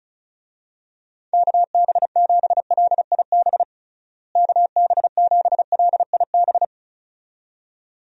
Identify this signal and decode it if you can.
Morse code, heard as a keyed tone, about 34 wpm, 700 Hz. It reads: KB7LIB KB7LIB
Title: USA Call Signs - 6 Characters - Repeated 2x - 35wpm